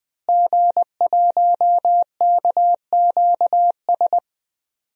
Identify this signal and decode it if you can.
Morse code, heard as a keyed tone, about 20 wpm, 700 Hz. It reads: G1KQS
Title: DXCC-100 Call Signs - 5 Characters - Repeated 2x - 20wpm